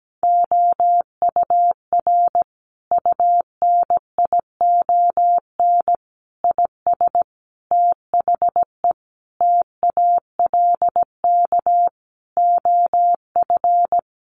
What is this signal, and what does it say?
Morse code, heard as a keyed tone, about 17 wpm, 700 Hz. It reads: OUR UNION IS THE TALK OF
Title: Sentences from Top 1400 Words 17wpm